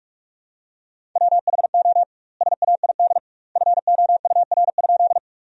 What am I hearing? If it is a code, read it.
Morse code, heard as a keyed tone, about 48 wpm, 700 Hz. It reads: WHO SAID FOUR?
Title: 2x Word-Spacing - Sentences from Top 300 Words - 45wpm